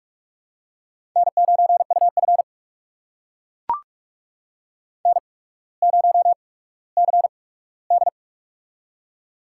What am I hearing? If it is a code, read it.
Morse code, heard as a keyed tone, about 44 wpm, spaced as for 19 wpm, 690 Hz. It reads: N9UF N0CD
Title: USA Call Signs - 4 Characters - 8x Character Spacing - 45wpm